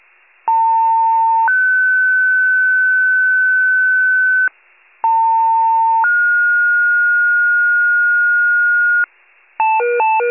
Two-tone page, heard as a one-second tone, then a three-second tone, about 900 then 1500 Hz, about 900 then 1430 Hz. It is heard from 0.5 to 9.0 s.